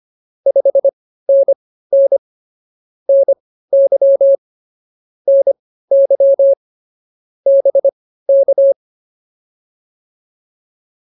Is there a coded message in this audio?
Yes: Morse code from 0.5 to 8.7 s, heard as a keyed tone, about 24 wpm, spaced as for 15 wpm, 550 Hz. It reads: 5NN NY NY BK